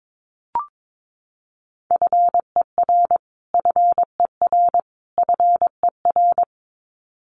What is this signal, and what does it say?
Morse code, heard as a keyed tone, about 22 wpm, 700 Hz. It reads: FER FER FER
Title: Abbreviations Repeated 3x - 22wpm